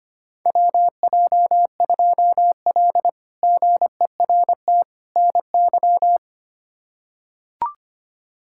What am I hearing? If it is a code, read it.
Morse code, heard as a keyed tone, about 25 wpm, 700 Hz. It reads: WJ2L GERT NY
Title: K1USN SST QSO Practice - 25wpm